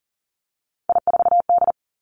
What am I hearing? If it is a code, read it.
Morse code, heard as a keyed tone, about 40 wpm, 700 Hz. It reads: I4D
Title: Three Challenging Letter-Number Combinations 40wpm